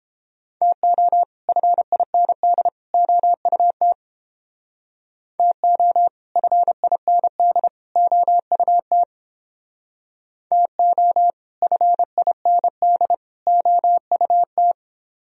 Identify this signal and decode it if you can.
Morse code, heard as a keyed tone, about 29 wpm, 700 Hz. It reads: TO FIND OUT TO FIND OUT TO FIND OUT